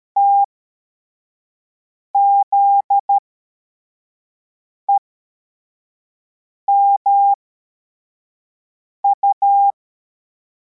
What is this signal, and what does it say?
Morse code, heard as a keyed tone, about 13 wpm, spaced as for 4 wpm, 800 Hz. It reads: TZEMU
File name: formation / cw / Text52